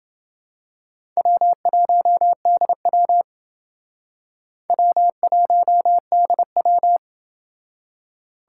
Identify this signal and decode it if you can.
Morse code, heard as a keyed tone, about 28 wpm, 700 Hz. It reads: W1DW W1DW